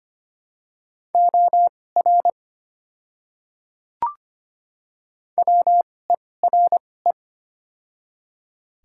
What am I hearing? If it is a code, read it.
Morse code, heard as a keyed tone, about 25 wpm, spaced as for 18 wpm, 700 Hz. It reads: OR WERE
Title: 2x Character Spacing - Top 100 Words - 25wpm